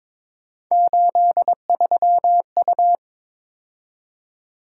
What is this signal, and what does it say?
Morse code, heard as a keyed tone, about 22 wpm, 700 Hz. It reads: 83U
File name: practice / 22wpm / Three Letter-Number Combinations - 22wpm